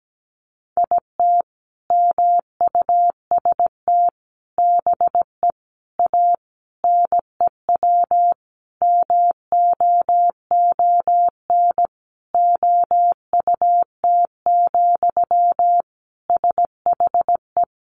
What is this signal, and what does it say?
Morse code, heard as a keyed tone, about 17 wpm, 700 Hz. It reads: IT MUST BE A NEW MOON OUT, SHE
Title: Sentences from Top 500 Words 17wpm